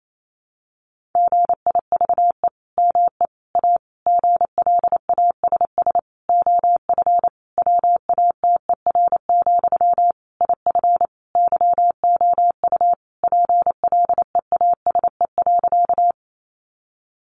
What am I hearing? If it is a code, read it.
Morse code, heard as a keyed tone, about 28 wpm, 700 Hz. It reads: GIVE ME A GLASS OF WATER, IF YOU PLEASE.
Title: Sentences from Top 1500 Words 28wpm